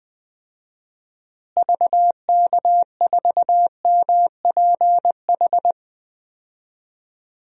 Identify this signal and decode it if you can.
Morse code, heard as a keyed tone, about 20 wpm, 700 Hz. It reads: VK4MPH
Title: DXCC-100 Call Signs - 6 Characters - Repeated 3x - 20wpm